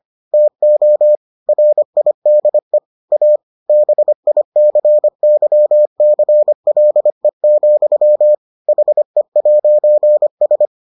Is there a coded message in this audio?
Yes: Morse code from 0.3 to 10.7 s, heard as a keyed tone, about 25 wpm, 600 Hz. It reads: TO RIDE A BICYCLE, HE'S